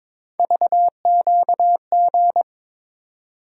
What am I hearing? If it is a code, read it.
Morse code, heard as a keyed tone, about 22 wpm, 700 Hz. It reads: VQG
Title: USA Call Sign Postfixes - 3 Letters 22wpm